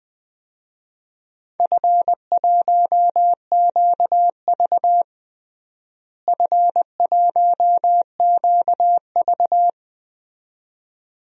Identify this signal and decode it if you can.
Morse code, heard as a keyed tone, about 20 wpm, 700 Hz. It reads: F1QV F1QV
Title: DXCC-100 Call Signs - 4 Characters - Repeated 2x - 20wpm